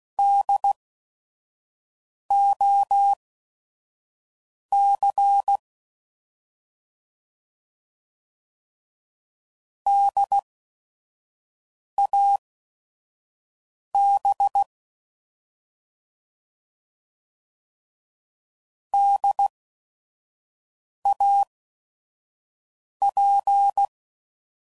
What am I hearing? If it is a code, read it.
Morse code, heard as a keyed tone, about 16 wpm, spaced as for 5 wpm, 800 Hz. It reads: DOC DAB DAP
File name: Prefixes / By First Letter / Ds